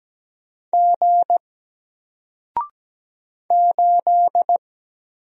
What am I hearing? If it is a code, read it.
Morse code, heard as a keyed tone, about 17 wpm, 700 Hz. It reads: G 8
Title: Single Character (Letter:Number:Punctuation:Prosign) Alt 17wpm